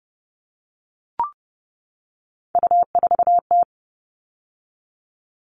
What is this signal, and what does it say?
Morse code, heard as a keyed tone, about 30 wpm, 700 Hz. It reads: U4T
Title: All Cut Numbers - 3-Digit 30wpm